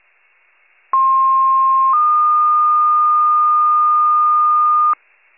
Two-tone page, heard as a one-second tone, then a three-second tone, about 1060 then 1220 Hz, with noise. It is heard from 0.9 to 4.9 s.